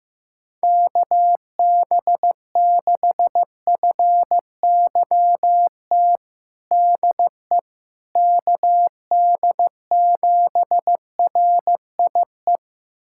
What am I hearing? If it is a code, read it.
Morse code, heard as a keyed tone, about 15 wpm, 700 Hz. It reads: KB6FYT DE KD7RIE
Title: USA Callsign DE Callsign - 6 Characters - 15wpm